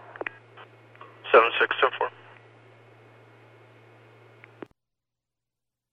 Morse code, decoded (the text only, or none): none